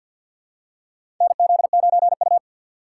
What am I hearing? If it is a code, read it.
Morse code, heard as a keyed tone, about 52 wpm, 700 Hz. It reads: NZ9U
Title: USA Call Signs - 4 Characters - 3x Character Spacing - 50wpm